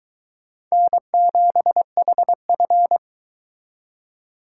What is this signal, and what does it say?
Morse code, heard as a keyed tone, about 23 wpm, 700 Hz. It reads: N7HF